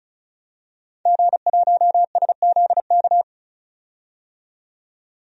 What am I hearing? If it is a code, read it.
Morse code, heard as a keyed tone, about 34 wpm, 700 Hz. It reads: G1SZK
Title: DXCC-100 Call Signs - 5 Characters - 35wpm